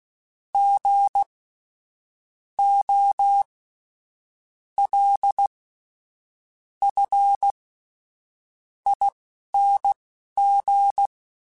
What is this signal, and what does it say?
Morse code, heard as a keyed tone, about 16 wpm, spaced as for 7 wpm, 800 Hz. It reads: GOLFING
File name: Spaced Alphas / Spaced Alphas mp3s / Gs sp @10